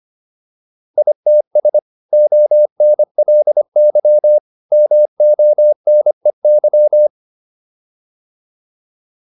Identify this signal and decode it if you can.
Morse code, heard as a keyed tone, about 25 wpm, 600 Hz. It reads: ITS ONLY MONEY